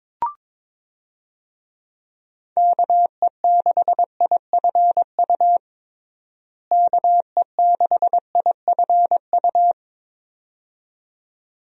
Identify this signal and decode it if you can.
Morse code, heard as a keyed tone, about 21 wpm, 700 Hz. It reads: KE6IFU KE6IFU
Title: USA Call Signs - 6 Characters - Repeated 2x - 22wpm